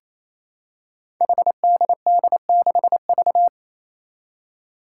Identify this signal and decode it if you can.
Morse code, heard as a keyed tone, about 27 wpm, 700 Hz. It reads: HDD6V
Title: Five Challenging Letter-Number Combinations 28wpm